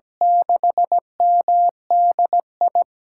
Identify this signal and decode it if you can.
Morse code, heard as a keyed tone, about 17 wpm, 700 Hz. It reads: 6MDI
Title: USA Call Signs - 6 Characters - 8x Character Spacing - 17wpm